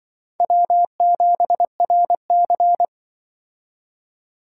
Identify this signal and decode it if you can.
Morse code, heard as a keyed tone, about 25 wpm, 700 Hz. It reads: W7RC
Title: Speed Racing - USA 1x2 Call Signs 20wpm